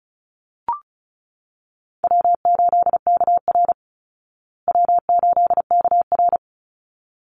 Morse code, heard as a keyed tone, about 34 wpm, 700 Hz. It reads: W8KR W8KR
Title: Easy Call Signs Repeated 2x - 35wpm